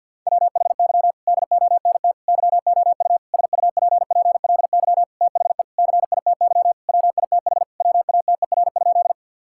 Morse code, heard as a keyed tone, about 50 wpm, 700 Hz. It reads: WHY DONT YOU SUPPLY THE CITY WITH WATER?
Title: Sentences from Top 1000 Words - Review Part 5 - 50wpm